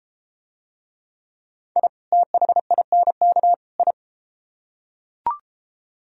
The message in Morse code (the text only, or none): I THINK I